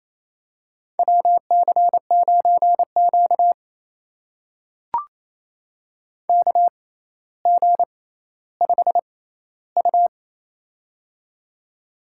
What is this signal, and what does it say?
Morse code, heard as a keyed tone, about 28 wpm, spaced as for 14 wpm, 700 Hz. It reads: WC9Q KG5U
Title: USA Call Signs - 6x Character Spacing - 28wpm